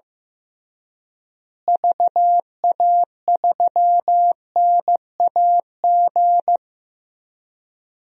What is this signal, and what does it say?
Morse code, heard as a keyed tone, about 15 wpm, 700 Hz. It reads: VA3NAG
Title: DXCC-100 Call Signs - 6 Characters - Repeated 3x - 15wpm